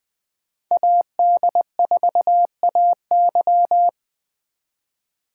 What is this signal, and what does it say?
Morse code, heard as a keyed tone, about 24 wpm, 700 Hz. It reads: AD4AY
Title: Speed Racing - USA 2x2 Call Signs 20wpm